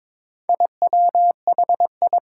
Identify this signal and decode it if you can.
Morse code, heard as a keyed tone, about 21 wpm, 700 Hz. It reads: IWHI
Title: Four Challenging Letter-Number Combinations 22wpm